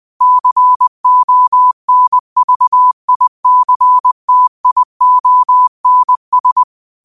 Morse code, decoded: CONVICTIONS

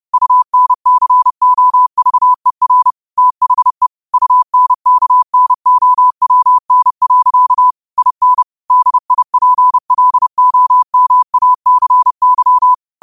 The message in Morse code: UNCOVER THE UNKNOWN. IN DIPLOMACY

